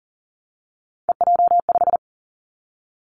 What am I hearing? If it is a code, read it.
Morse code, heard as a keyed tone, about 40 wpm, 690 Hz. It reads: EJ5
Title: Three Challenging Letter-Number Combinations 40wpm